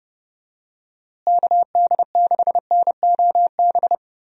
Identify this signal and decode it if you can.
Morse code, heard as a keyed tone, about 32 wpm, 700 Hz. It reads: KD6NOB
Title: USA Call Signs - 6 Characters - Repeated 3x - 30wpm